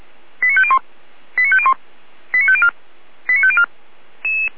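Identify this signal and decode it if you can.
Five-tone selective call (ZVEI), heard as quick runs of five tones, tones of about 70 ms. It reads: 89501 89501 89504 89504
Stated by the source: ZVEI five-tone selective call received by SDR, fire brigade Orlamünde (feuerwehr-orlamuende) server.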